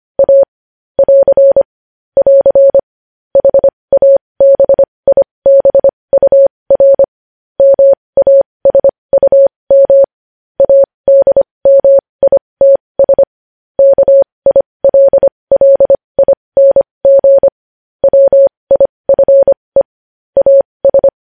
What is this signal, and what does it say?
Morse code, heard as a keyed tone, about 25 wpm, spaced as for 20 wpm, 550 Hz. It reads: A <AR> <AR> HABIBUR MASUM ADMITS KILLING WIFE AS
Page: news-headlines-in-morse-code-at-20-wpm